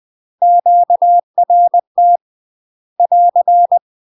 Morse code, heard as a keyed tone, about 20 wpm, 700 Hz. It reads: QRT <AR>
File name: UK MASTER QSOs TRAINING / 20WPM / UK MASTER QSOs 0072